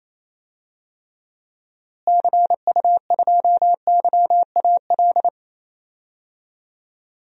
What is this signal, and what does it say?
Morse code, heard as a keyed tone, about 28 wpm, 700 Hz. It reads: CU2YAL
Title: DXCC-100 Call Signs - 6 Characters - Repeated 2x - 28wpm